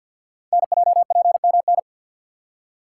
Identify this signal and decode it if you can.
Morse code, heard as a keyed tone, about 51 wpm, 700 Hz. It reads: NJPMN
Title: Code Groups - 5 Letters - 50wpm